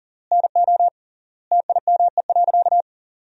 Morse code, heard as a keyed tone, about 40 wpm, 700 Hz. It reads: NO TIME.
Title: 3x Word-Spacing - Sentences from Top 400 Words - 40wpm